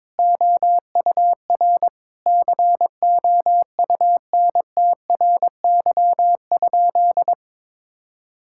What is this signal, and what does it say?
Morse code, heard as a keyed tone, about 22 wpm, 700 Hz. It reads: OUR COUNTRY?
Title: Sentences from Top 1000 Words - Review Part 5 - 22wpm